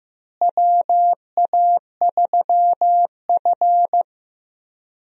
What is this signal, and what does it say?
Morse code, heard as a keyed tone, about 15 wpm, 700 Hz. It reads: WA3F
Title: USA Call Signs - 4 Characters - Repeated 3x - 15wpm